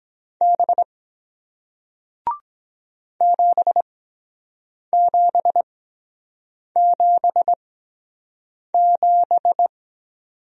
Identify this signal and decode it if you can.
Morse code, heard as a keyed tone, about 23 wpm, spaced as for 7 wpm, 700 Hz. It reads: B 7777